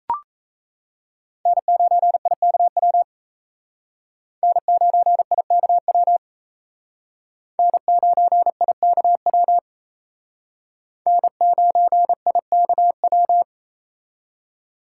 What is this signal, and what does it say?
Morse code, heard as a keyed tone, about 35 wpm, 700 Hz. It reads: N9IKW N9IKW N9IKW N9IKW